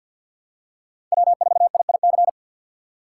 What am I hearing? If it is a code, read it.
Morse code, heard as a keyed tone, about 51 wpm, 700 Hz. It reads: W4IIC
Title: K1USN SST QSO Practice - 50wpm